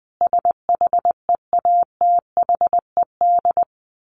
Morse code, heard as a keyed tone, about 20 wpm, 700 Hz. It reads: SHEATHED
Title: A Sound Of Thunder 20wpm